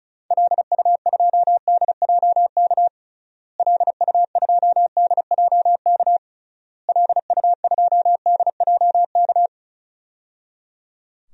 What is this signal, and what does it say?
Morse code, heard as a keyed tone, about 34 wpm, 700 Hz. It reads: LU2DJK LU2DJK LU2DJK
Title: DXCC-100 Call Signs - 6 Characters - Repeated 3x - 35wpm